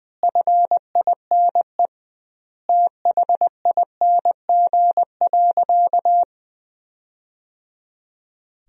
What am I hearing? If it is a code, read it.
Morse code, heard as a keyed tone, about 20 wpm, 700 Hz. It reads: FINE THING.
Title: 2x Word-Spacing - Sentences from Top 500 Words - 20wpm